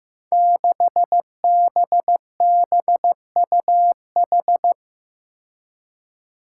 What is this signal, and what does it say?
Morse code, heard as a keyed tone, about 15 wpm, 700 Hz. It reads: 6BBUH
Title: Five Challenging Letter-Number Combinations 15wpm